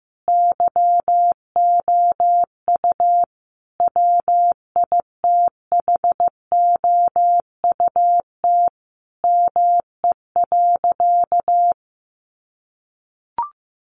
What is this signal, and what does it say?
Morse code, heard as a keyed tone, about 15 wpm, 700 Hz. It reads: YOU WITHOUT ME.
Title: Sentences from Top 1200 Words 15wpm